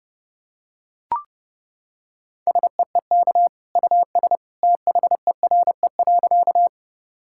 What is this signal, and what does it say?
Morse code, heard as a keyed tone, about 32 wpm, 700 Hz. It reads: SEEK US THERE.
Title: Sentences from Top 1600 Words 30wpm